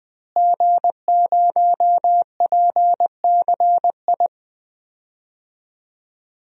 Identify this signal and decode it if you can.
Morse code, heard as a keyed tone, about 20 wpm, 700 Hz. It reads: G0PCI
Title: DXCC-100 Call Signs - 5 Characters - Repeated 3x - 20wpm